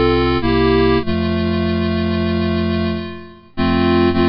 There is no code.